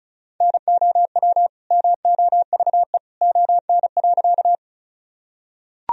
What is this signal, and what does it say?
Morse code, heard as a keyed tone, about 36 wpm, 700 Hz. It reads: NOW MOVE ON.
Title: Sentences from Top 1000 Words - Review Part 1 - 35wpm